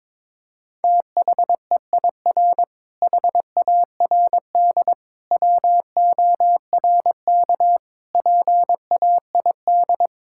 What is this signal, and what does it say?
Morse code, heard as a keyed tone, about 22 wpm, 700 Hz. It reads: THEIR HARD WORK PAID
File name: Difficult Sentences with Hints - 22wpm